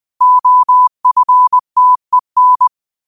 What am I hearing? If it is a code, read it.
Morse code, heard as a keyed tone, about 20 wpm, 1000 Hz. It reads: OFTEN
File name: qotdfiles / 20wpm